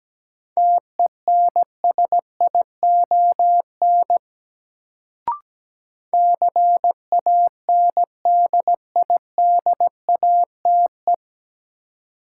Morse code, heard as a keyed tone, about 17 wpm, 700 Hz. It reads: TENSION CANDIDATE
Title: Sentences from Top 1600 Words - New Word Drill - 17wpm